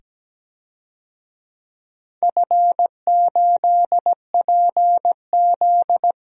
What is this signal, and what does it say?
Morse code, heard as a keyed tone, about 17 wpm, 700 Hz. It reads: F8PZ
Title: DXCC-100 Call Signs - 4 Characters - 17wpm